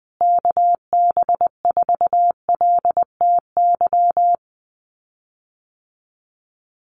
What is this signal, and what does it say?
Morse code, heard as a keyed tone, about 19 wpm, 700 Hz. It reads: KB4LTY
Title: USA Call Signs Repeated 2x 20wpm